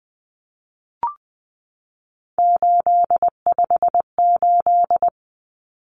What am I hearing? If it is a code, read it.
Morse code, heard as a keyed tone, about 20 wpm, 700 Hz. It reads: 858
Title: Numbers - 3-Digit 20wpm